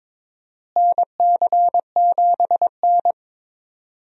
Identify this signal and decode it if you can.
Morse code, heard as a keyed tone, about 22 wpm, 700 Hz. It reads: NC7N
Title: USA Call Signs - 4 Characters - Repeated 3x - 22wpm